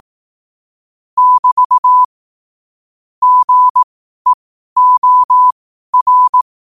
Morse code, heard as a keyed tone, about 18 wpm, spaced as for 12 wpm, 1000 Hz. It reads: <BT> GEOR
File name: qotdfiles / 12wpm